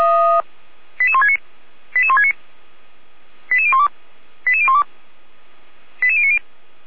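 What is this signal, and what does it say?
Five-tone selective call (ZVEI), heard as quick runs of five tones, tones of about 70 ms. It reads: 88169 88169 88012 88012 88090
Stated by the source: Five-tone ZVEI selective call 88367, received via SDR.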